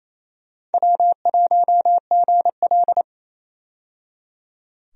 Morse code, heard as a keyed tone, about 28 wpm, 700 Hz. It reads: W1GL
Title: USA Call Signs - 4 Characters - 28wpm